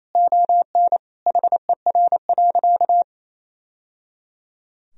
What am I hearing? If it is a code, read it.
Morse code, heard as a keyed tone, about 28 wpm, 700 Hz. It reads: ON HER.
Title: Sentences from Top 1000 Words - Review Part 1 - 28wpm